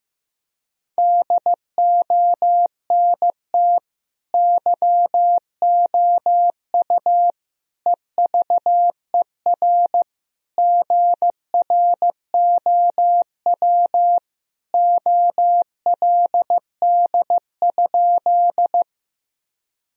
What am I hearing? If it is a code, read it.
Morse code, heard as a keyed tone, about 15 wpm, 700 Hz. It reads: DONT YOU EVER GROW OLD?
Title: Sentences from Top 1000 Words - Review Part 1 - 15wpm